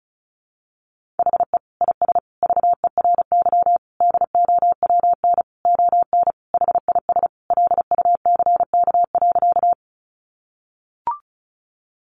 Morse code, heard as a keyed tone, about 34 wpm, 700 Hz. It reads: HE IS VERY DOWN ON HIS LUCK.
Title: Sentences from Top 1500 Words 35wpm